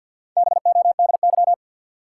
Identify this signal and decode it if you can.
Morse code, heard as a keyed tone, about 52 wpm, 700 Hz. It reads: BODY
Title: Sentences from Top 400 Words - New Word Drill - 50wpm